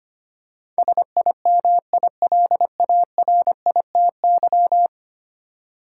Morse code, heard as a keyed tone, about 25 wpm, 700 Hz. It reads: SIMILARITY
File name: 10 Letter Words - First 3 Letters Given - 25wpm